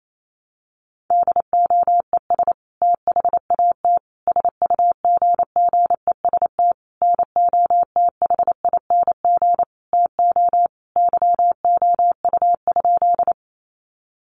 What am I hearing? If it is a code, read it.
Morse code, heard as a keyed tone, about 28 wpm, 700 Hz. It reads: DOES THAT SUGGEST NOTHING TO YOU?
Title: Sentences from Top 1300 Words 28wpm